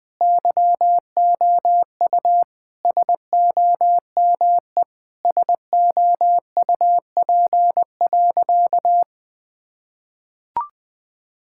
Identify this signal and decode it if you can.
Morse code, heard as a keyed tone, about 20 wpm, 700 Hz. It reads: YOU SOME SOUP.
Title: Sentences from Top 1000 Words - Review Part 6 - 20wpm